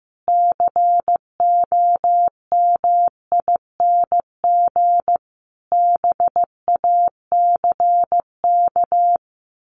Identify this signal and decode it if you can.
Morse code, heard as a keyed tone, about 15 wpm, 700 Hz. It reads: COMING BACK
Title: A Sound Of Thunder 15wpm